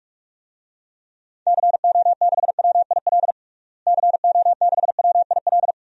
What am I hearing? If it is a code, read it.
Morse code, heard as a keyed tone, about 46 wpm, 700 Hz. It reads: CO6WIL CO6WIL
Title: DXCC-100 Call Signs - 6 Characters - Repeated 2x - 45wpm